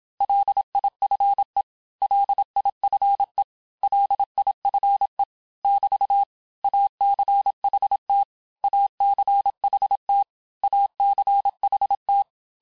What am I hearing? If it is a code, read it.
Morse code, heard as a keyed tone, about 27 wpm, 800 Hz. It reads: LIFE LIFE LIFE <BT> ACHT ACHT ACHT
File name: cours-du-mercredi / audios / cours-869-texte-LES-MOTS-vitesse-25-MPM